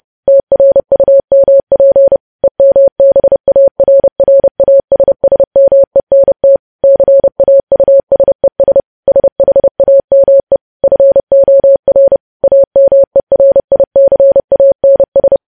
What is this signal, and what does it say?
Morse code, heard as a keyed tone, about 30 wpm, 550 Hz. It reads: TRUMP EMBARRASSMENT CAUSES SHAME FOR AMERICANS
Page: news headlines in morse code at 30 wpm